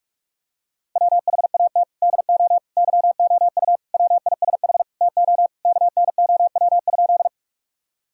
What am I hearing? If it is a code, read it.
Morse code, heard as a keyed tone, about 46 wpm, 700 Hz. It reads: WHAT DO YOU WISH TO KNOW?
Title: Sentences from Top 1600 Words 45wpm